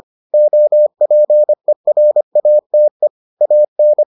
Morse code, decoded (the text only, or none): OPERATE AN